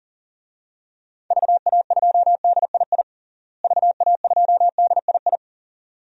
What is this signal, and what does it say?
Morse code, heard as a keyed tone, about 37 wpm, 700 Hz. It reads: VA2DII VA2DII
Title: DXCC-100 Call Signs - 6 Characters - Repeated 2x - 40wpm